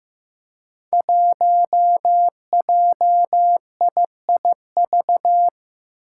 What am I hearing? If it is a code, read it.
Morse code, heard as a keyed tone, about 15 wpm, 700 Hz. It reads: 1JIIV
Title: Five Challenging Letter-Number Combinations 15wpm